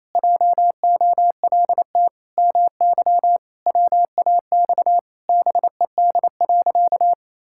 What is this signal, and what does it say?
Morse code, heard as a keyed tone, about 28 wpm, 700 Hz. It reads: JOLT MY WAX BED.
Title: Pangram Challenge v2 - 28wpm